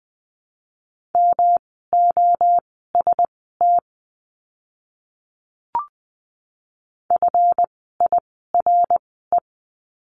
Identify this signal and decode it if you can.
Morse code, heard as a keyed tone, about 20 wpm, spaced as for 14 wpm, 700 Hz. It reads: MOST FIRE
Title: 139 Lesson 23f - Letter f - Words 20wpm